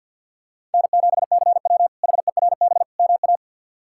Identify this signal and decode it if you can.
Morse code, heard as a keyed tone, about 51 wpm, 700 Hz. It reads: N7CW HERB MA
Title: K1USN SST QSO Practice - 50wpm